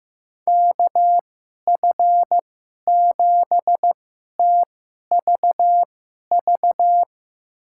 Morse code, heard as a keyed tone, about 15 wpm, spaced as for 11 wpm, 700 Hz. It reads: KF7TVV
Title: USA Call Signs - 6 Characters - 2x Character Spacing - 15wpm